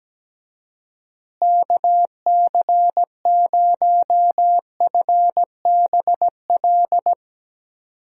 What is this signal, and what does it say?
Morse code, heard as a keyed tone, about 17 wpm, 700 Hz. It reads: KC0FBL
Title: USA Call Signs - 6 Characters - Repeated 2x - 17wpm